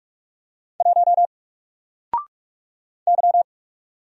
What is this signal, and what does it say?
Morse code, heard as a keyed tone, about 44 wpm, 700 Hz. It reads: 1 Y
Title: Single Letter-Number 45wpm